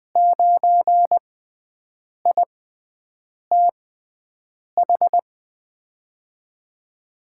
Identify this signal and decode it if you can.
Morse code, heard as a keyed tone, about 20 wpm, spaced as for 7 wpm, 700 Hz. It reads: 9ITH
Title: USA Call Signs - 6 Characters - 6x Character Spacing - 20wpm